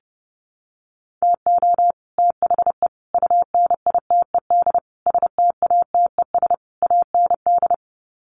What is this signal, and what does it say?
Morse code, heard as a keyed tone, about 30 wpm, 700 Hz. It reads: TO THE UNITED STATES AND
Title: Sets of 5 Words Encore 30wpm